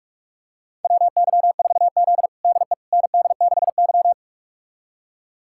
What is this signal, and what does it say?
Morse code, heard as a keyed tone, about 44 wpm, 700 Hz. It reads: WY4Z DE ND6Y
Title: USA Callsign DE Callsign - 4 Characters - 45wpm